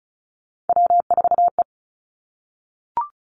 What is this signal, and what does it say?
Morse code, heard as a keyed tone, about 36 wpm, 700 Hz. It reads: W4E
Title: Three Challenging Letter-Number Combinations 35wpm